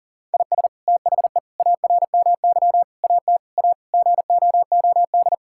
Morse code, heard as a keyed tone, about 40 wpm, 700 Hz. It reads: IS THE ARMY AT A GOOD
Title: Sentences from Top 1000 Words - Review Part 6 - 40wpm